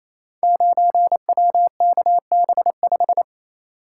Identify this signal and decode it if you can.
Morse code, heard as a keyed tone, about 27 wpm, 700 Hz. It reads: 9WKB5
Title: Code Groups - 5 Letter:Numbers - 28wpm